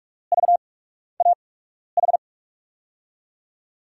Morse code, heard as a keyed tone, about 44 wpm, spaced as for 12 wpm, 700 Hz. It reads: VAH
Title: USA Call Signs - 8x Character Spacing - 45wpm